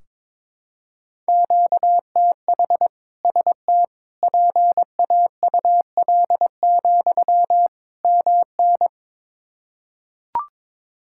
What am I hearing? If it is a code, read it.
Morse code, heard as a keyed tone, about 22 wpm, 700 Hz. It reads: QTH ST PAUL, MN
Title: K1USN SST QSO Practice 22wpm